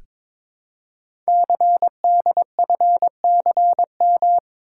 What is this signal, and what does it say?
Morse code, heard as a keyed tone, about 22 wpm, 700 Hz. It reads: CDFCM
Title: Code Groups - 5 Letters - 22wpm